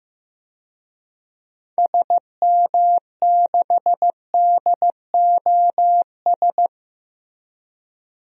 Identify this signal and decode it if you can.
Morse code, heard as a keyed tone, about 15 wpm, 700 Hz. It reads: SM6DOS
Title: DXCC-100 Call Signs - 6 Characters - Repeated 2x - 15wpm